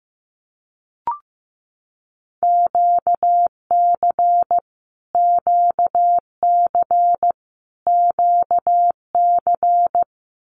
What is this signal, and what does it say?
Morse code, heard as a keyed tone, about 15 wpm, 700 Hz. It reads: QC QC QC
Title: Canadian Provinces and Territories - Repeated 3x - 15wpm